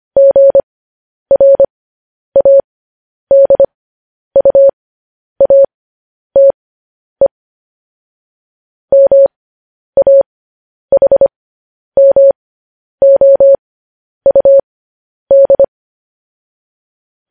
Morse code, heard as a keyed tone, about 25 wpm, spaced as for 10 wpm, 550 Hz. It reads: GRADUATE MAHMOUD
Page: news headlines in morse code at 10 wpm